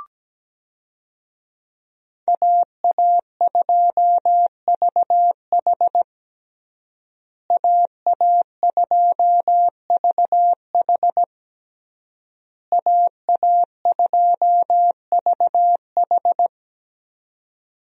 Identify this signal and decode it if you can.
Morse code, heard as a keyed tone, about 17 wpm, 700 Hz. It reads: AA2VH AA2VH AA2VH
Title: USA Call Signs - 5 Characters - Repeated 3x - 17wpm